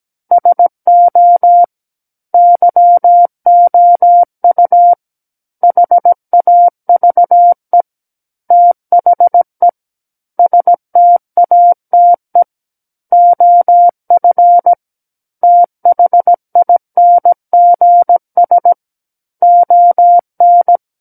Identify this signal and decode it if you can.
Morse code, heard as a keyed tone, about 17 wpm, 700 Hz. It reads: SO YOU HAVE THE STATE OF THINGS ON
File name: Book One / 08-Chapter 8 - 17 WPM